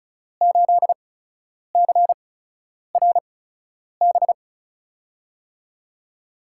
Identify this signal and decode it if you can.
Morse code, heard as a keyed tone, about 36 wpm, spaced as for 10 wpm, 700 Hz. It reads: 8CRB
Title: USA Call Signs - 6 Characters - 8x Character Spacing - 35wpm